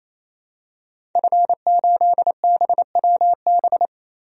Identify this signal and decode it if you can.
Morse code, heard as a keyed tone, about 28 wpm, 700 Hz. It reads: F8BWB
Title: DXCC-100 Call Signs - 5 Characters - 28wpm